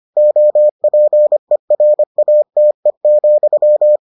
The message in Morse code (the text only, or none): OPERATE,